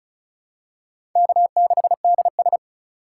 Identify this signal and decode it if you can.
Morse code, heard as a keyed tone, about 36 wpm, 700 Hz. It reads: K6DS
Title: USA Call Signs - 4 Characters - 3x Character Spacing - 35wpm